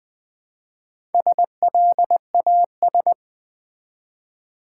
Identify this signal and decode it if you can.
Morse code, heard as a keyed tone, about 20 wpm, 700 Hz. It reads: SLAS
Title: Build A Word - 1 Letter At A Time - 3 to 5-Letter Words - 20wpm